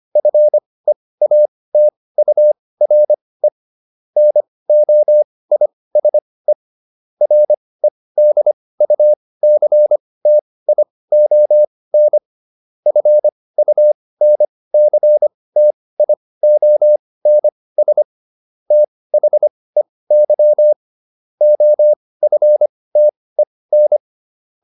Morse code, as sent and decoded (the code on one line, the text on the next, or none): ..-. .- - ..- .-. / -. --- .. ... / .-. -.. ..- -.-. - .. --- -. / ..-. ..- -. -.-. - .. --- -. ... / - .... -.-- / --- ..-. - . -.
FEATURE NOISE REDUCTION FUNCTIONS THEY OFTEN